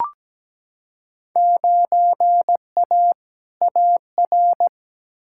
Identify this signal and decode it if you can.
Morse code, heard as a keyed tone, about 17 wpm, 700 Hz. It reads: 9A AR
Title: Field Day Exchanges Encore - 17wpm